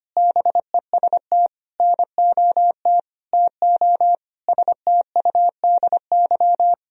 Morse code, heard as a keyed tone, about 25 wpm, 700 Hz. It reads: BEST NOT TO STUDY
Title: Sentences from Top 1000 Words - Review Part 1 - 25wpm